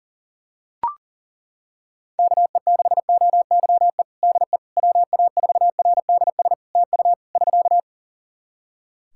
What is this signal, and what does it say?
Morse code, heard as a keyed tone, about 40 wpm, 700 Hz. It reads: KE6OYE DE WA4RDS TU <SK>
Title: Practice QSO - Exchange 4 - Ending - 40wpm